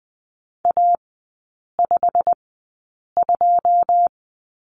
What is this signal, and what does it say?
Morse code, heard as a keyed tone, about 20 wpm, spaced as for 8 wpm, 700 Hz. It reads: A52
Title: 107 Lesson 18e - Letter m - Three Characters 20wpm